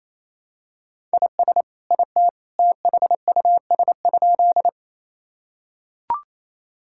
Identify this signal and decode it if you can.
Morse code, heard as a keyed tone, about 28 wpm, 700 Hz. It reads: IS IT THUS?
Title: Sentences from Top 1000 Words - Review Part 5 - 28wpm